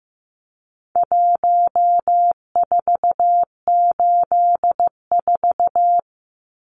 Morse code, heard as a keyed tone, about 15 wpm, 700 Hz. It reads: 1484
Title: Numbers - Contest Style - 4-Digit 15wpm